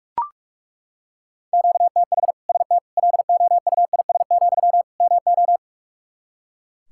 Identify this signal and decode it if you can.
Morse code, heard as a keyed tone, about 41 wpm, 700 Hz. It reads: QTH ST LOUIS, MO
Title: Practice QSO - Exchange 1 - Beginning - 45wpm